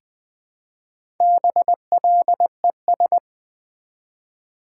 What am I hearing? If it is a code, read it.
Morse code, heard as a keyed tone, about 20 wpm, 700 Hz. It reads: BLES
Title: Build A Word - 1 Letter At A Time - 5-Letter Words - 20wpm